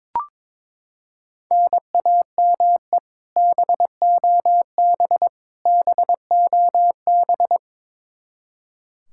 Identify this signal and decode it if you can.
Morse code, heard as a keyed tone, about 22 wpm, 700 Hz. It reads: NAME BOB BOB
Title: K1USN SST QSO Practice 22wpm